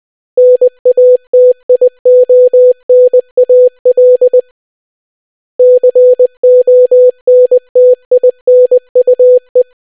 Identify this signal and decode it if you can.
Morse code, heard as a keyed tone, about 20 wpm, 500 Hz. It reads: NATIONAL CONTINUE